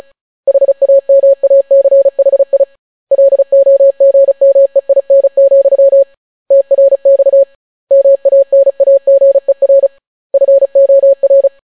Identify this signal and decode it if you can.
Morse code, heard as a keyed tone, about 36 wpm, 550 Hz. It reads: HAMACHI LOGMEIN, TRX MANAGER FOR